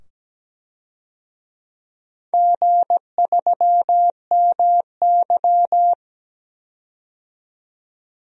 Morse code, heard as a keyed tone, about 17 wpm, 700 Hz. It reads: G3MY
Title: DXCC-100 Call Signs - 4 Characters - 17wpm